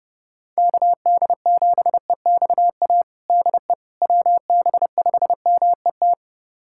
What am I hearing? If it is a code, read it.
Morse code, heard as a keyed tone, about 30 wpm, 700 Hz. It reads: KD7EXA DE WB5MET
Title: USA Callsign DE Callsign - 6 Characters - 30wpm